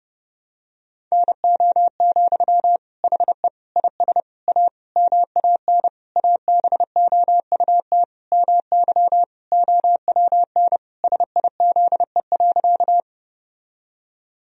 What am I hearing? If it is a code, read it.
Morse code, heard as a keyed tone, about 30 wpm, 700 Hz. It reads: NO, HE IS A MAN ABOUT MY OWN SIZE.
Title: Sentences from Top 1000 Words - Review Part 2 - 30wpm